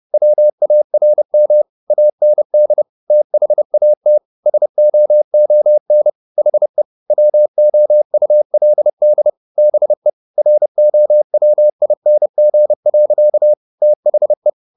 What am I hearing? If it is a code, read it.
Morse code, heard as a keyed tone, about 30 wpm, 600 Hz. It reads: WARM AND THAT SOON HE WOULD BE ROWING. THE
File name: the old man and the sea / the old man and the sea 1 30wpm